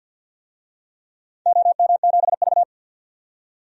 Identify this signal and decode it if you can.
Morse code, heard as a keyed tone, about 50 wpm, 700 Hz. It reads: OM7V
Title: DXCC-100 Call Signs - 4 Characters - 50wpm